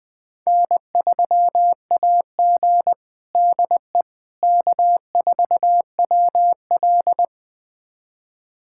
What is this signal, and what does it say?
Morse code, heard as a keyed tone, about 20 wpm, 700 Hz. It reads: N3AG DE K4WL
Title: USA Callsign DE Callsign - 4 Characters - 20wpm